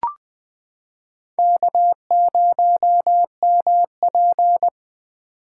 Morse code, heard as a keyed tone, about 21 wpm, 700 Hz. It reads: K0MP
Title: 2023 SOTA Chaser Call Signs - 20wpm